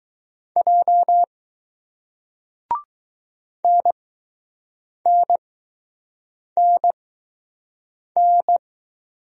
Morse code, heard as a keyed tone, about 20 wpm, spaced as for 6 wpm, 700 Hz. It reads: J NNNN